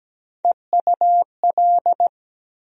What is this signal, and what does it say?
Morse code, heard as a keyed tone, about 17 wpm, 700 Hz. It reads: EUL
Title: USA Call Sign Postfixes - 3 Letters 17wpm